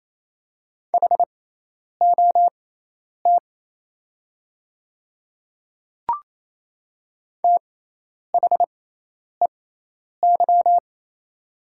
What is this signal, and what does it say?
Morse code, heard as a keyed tone, about 27 wpm, spaced as for 10 wpm, 700 Hz. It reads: HOT THEY
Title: 6x Character Spacing - Top 100 Words - 28wpm